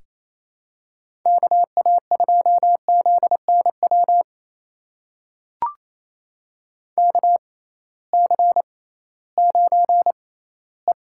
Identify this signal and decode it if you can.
Morse code, heard as a keyed tone, about 28 wpm, spaced as for 16 wpm, 700 Hz. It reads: KA2ZNW KC9E
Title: USA Call Signs - 6 Characters - 6x Character Spacing - 28wpm